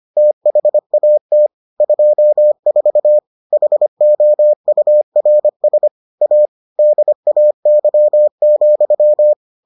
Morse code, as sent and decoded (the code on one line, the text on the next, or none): - .... .- - / ..--- ....- / .... --- ..- .-. ... / .- / -.. .- -.-- --..--
THAT 24 HOURS A DAY,